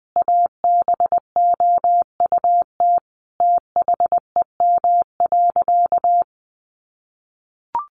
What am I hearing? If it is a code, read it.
Morse code, heard as a keyed tone, about 20 wpm, 700 Hz. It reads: ABOUT THEM.
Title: Sentences from Top 800 Words 20wpm